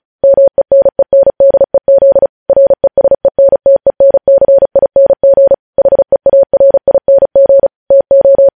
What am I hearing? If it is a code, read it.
Morse code, heard as a keyed tone, about 36 wpm, 550 Hz. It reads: MENENDEZ RESENTENCING HEARING TO
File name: podcast / News headlines-1x-35x35